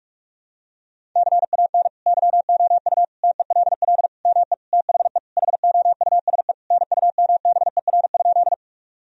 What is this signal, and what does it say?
Morse code, heard as a keyed tone, about 41 wpm, 700 Hz. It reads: CAN YOU TELL ME THE HOUSE NUMBER?
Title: Sentences from Top 1000 Words - Review Part 1 - 45wpm